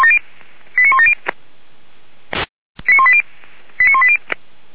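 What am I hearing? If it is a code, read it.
Five-tone selective call (ZVEI), heard as quick runs of five tones, tones of about 70 ms. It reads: 89170 89180 89180
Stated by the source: ZVEI five-tone selective call received on SDR, Feuerwehr Orlamünde.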